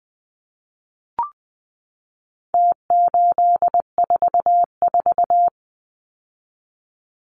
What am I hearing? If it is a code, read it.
Morse code, heard as a keyed tone, about 20 wpm, 700 Hz. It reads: T844
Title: Numbers - Contest Style - 4-Digit 20wpm